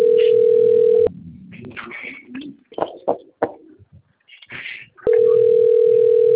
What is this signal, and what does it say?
Ringback tone (North America), 440 + 480 Hz.